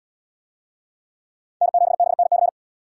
Morse code, heard as a keyed tone, about 77 wpm, 700 Hz. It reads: N7BNC